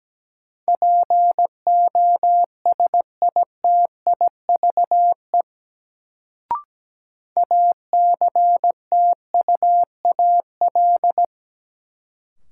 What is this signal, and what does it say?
Morse code, heard as a keyed tone, about 17 wpm, 700 Hz. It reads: POSITIVE ACTUAL
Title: Sentences from Top 1000 Words - Review Part 5 - New Word Drill - 17wpm